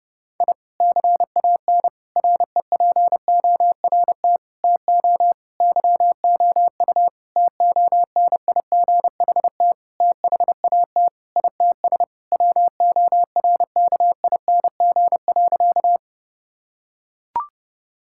Morse code, heard as a keyed tone, about 30 wpm, 700 Hz. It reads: I CAN REPORT TO YOU TONIGHT THAT ITS WORKING.
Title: Sentences from Top 1000 Words - Review Part 6 - 30wpm